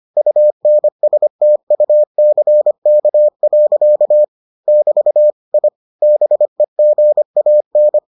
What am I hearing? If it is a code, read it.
Morse code, heard as a keyed tone, about 25 wpm, 600 Hz. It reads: UNSTUCK. <BT> I BEGAN